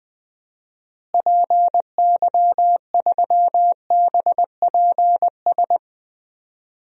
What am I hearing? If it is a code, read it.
Morse code, heard as a keyed tone, about 20 wpm, 700 Hz. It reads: PY3BPS
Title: DXCC-100 Call Signs - 6 Characters - Repeated 2x - 20wpm